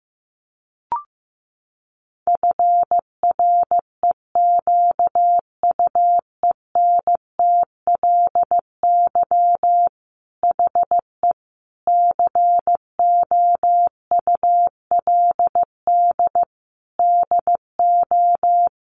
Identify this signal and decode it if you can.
Morse code, heard as a keyed tone, about 15 wpm, 700 Hz. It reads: FREQUENTLY HE COULD DO
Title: Sentences from Top 1300 Words 15wpm